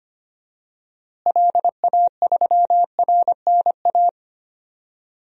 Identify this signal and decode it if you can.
Morse code, heard as a keyed tone, about 25 wpm, 700 Hz. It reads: LA3RNA
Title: DXCC-100 Call Signs - 6 Characters - Repeated 2x - 25wpm